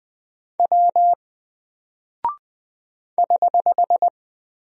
Morse code, heard as a keyed tone, about 20 wpm, 700 Hz. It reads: W <HH>
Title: Single Character (Letter:Number:Punctuation:Prosign) Alt 20wpm